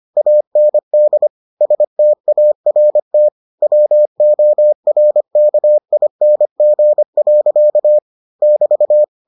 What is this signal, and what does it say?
Morse code, heard as a keyed tone, about 25 wpm, 600 Hz. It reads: AND START WORKING. <BT>